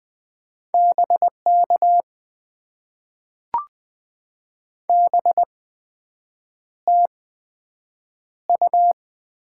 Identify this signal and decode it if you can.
Morse code, heard as a keyed tone, about 20 wpm, spaced as for 7 wpm, 700 Hz. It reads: BK BTU
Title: Top 90 QSO Elements - 8x Character Spacing - 20wpm